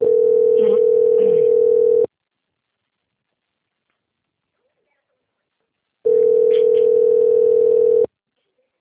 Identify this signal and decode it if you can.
Ringback tone (North America), 440 + 480 Hz.
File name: out- - - - - .36593